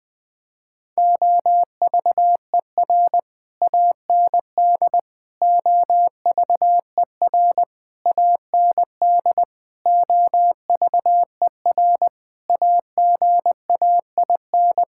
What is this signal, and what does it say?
Morse code, heard as a keyed tone, about 20 wpm, 700 Hz. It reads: OVER AND OVER AND OVER AGAIN
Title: Sets of 6 Words - 20wpm